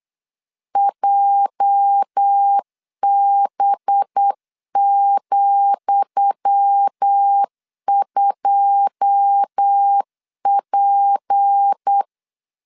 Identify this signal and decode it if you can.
Morse code, heard as a keyed tone, about 8 wpm, 800 Hz. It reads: JB,2P
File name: LEZIONI / 10 SIGNS / 10D COMA 8wpm 3749